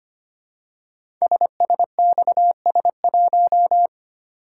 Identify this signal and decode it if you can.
Morse code, heard as a keyed tone, about 25 wpm, 700 Hz. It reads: SSXS1
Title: Code Groups - 5 Letter:Numbers - 25wpm